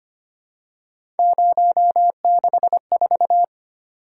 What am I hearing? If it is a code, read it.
Morse code, heard as a keyed tone, about 25 wpm, 700 Hz. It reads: 064
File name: Three Letter-Number Combinations - 25wpm